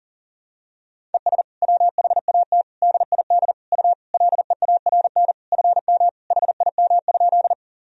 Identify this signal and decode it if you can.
Morse code, heard as a keyed tone, about 40 wpm, 700 Hz. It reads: ES WHAT DID U LEARN FM HIM?
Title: Sentences from Top 300 Words and Abbreviations 40wpm